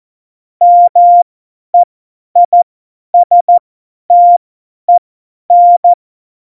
Morse code, heard as a keyed tone, about 14 wpm, spaced as for 10 wpm, 700 Hz. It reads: MEISTEN
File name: Morsekurs / Witze / Witz 26. November 2019, 50 F70